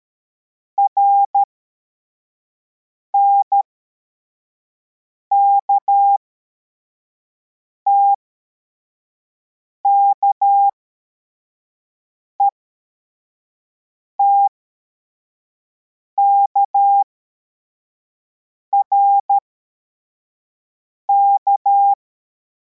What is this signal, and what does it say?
Morse code, heard as a keyed tone, about 13 wpm, spaced as for 4 wpm, 800 Hz. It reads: RNKTK ETKRK